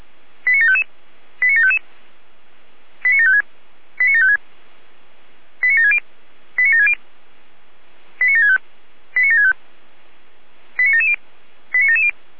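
Five-tone selective call (ZVEI), heard as quick runs of five tones, tones of about 80 ms. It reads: 89755 89755 89756 89756 89760 89760 89765 89765 89770 89770
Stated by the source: Five-tone ZVEI selective call received on SDR.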